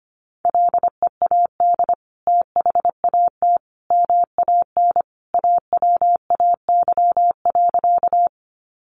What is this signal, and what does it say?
Morse code, heard as a keyed tone, about 25 wpm, 700 Hz. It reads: LEAD THAT MAN AWAY.
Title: Sentences from Top 600 Words 25wpm